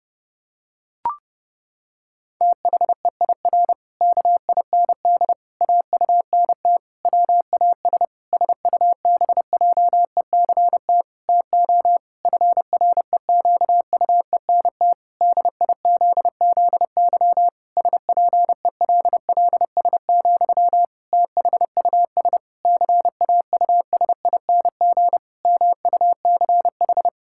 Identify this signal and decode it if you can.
Morse code, heard as a keyed tone, about 32 wpm, 700 Hz. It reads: THEIR KIND AUNT WAS SUBJECT TO FREQUENT DIZZY SPELLS, THUS CAUSING MUCH
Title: Pangram Challenge v2 - 30wpm